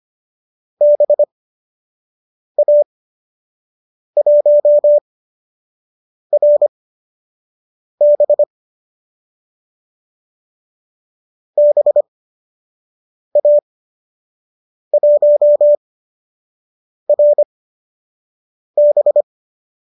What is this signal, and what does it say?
Morse code, heard as a keyed tone, about 25 wpm, spaced as for 6 wpm, 590 Hz. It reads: BA1RB BA1RB
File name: qso002 06